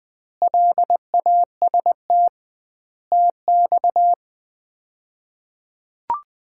Morse code, heard as a keyed tone, about 20 wpm, 700 Hz. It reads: LAST TX
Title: Common QSO Phrases - 2x Word-Spacing - 20wpm